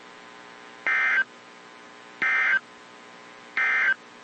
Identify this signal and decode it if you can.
Emergency Alert System (SAME) tones, heard as harsh buzzing bursts. It is heard from 0.8 to 4.0 s.
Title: WZ2523 - NOAA Weather Radio